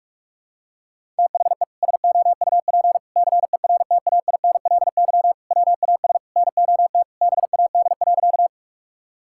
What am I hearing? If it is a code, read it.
Morse code, heard as a keyed tone, about 41 wpm, 700 Hz. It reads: THE SOUP CERTAINLY WAS NOT BAD.
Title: Sentences from Top 1000 Words - Review Part 6 - 45wpm